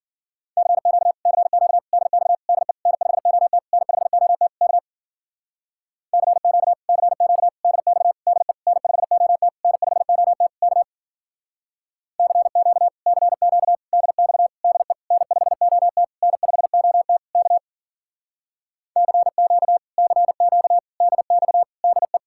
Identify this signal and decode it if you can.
Morse code, heard as a keyed tone, about 51 wpm, 700 Hz. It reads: CQ CQ DX DE N5OT N5OT K CQ CQ DX DE N5OT N5OT K CQ CQ DX DE N5OT N5OT K CQ CQ DX DE